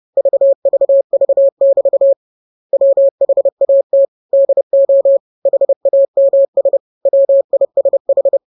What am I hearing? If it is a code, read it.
Morse code, heard as a keyed tone, about 30 wpm, 550 Hz. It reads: VVV<BT> WHAT DO HAMS WISH
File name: jokes / joke 17 30 0000